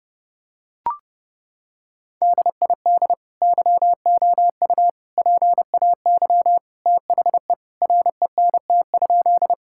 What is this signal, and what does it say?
Morse code, heard as a keyed tone, about 32 wpm, 700 Hz. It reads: DID YOU PAY THE RENT?
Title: Sentences from Top 1000 Words - Review Part 5 - 30wpm